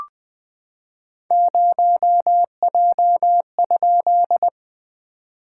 Morse code, heard as a keyed tone, about 21 wpm, 690 Hz. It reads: 0J?
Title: SOTA QSO Practice 20wpm